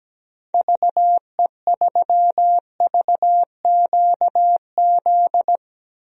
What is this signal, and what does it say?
Morse code, heard as a keyed tone, about 17 wpm, 700 Hz. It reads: VE3VQZ
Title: DXCC-100 Call Signs - 6 Characters - Repeated 2x - 17wpm